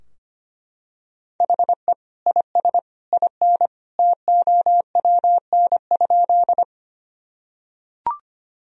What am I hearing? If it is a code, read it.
Morse code, heard as a keyed tone, about 25 wpm, 700 Hz. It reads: HE IS IN TOWN?
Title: Sentences from Top 1000 Words - Review Part 2 - 25wpm